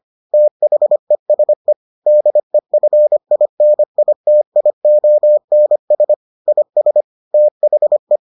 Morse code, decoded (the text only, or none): THESE DEFINITIONS IS THE